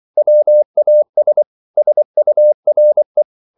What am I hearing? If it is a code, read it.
Morse code, heard as a keyed tone, about 24 wpm, 600 Hz. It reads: WAS SURE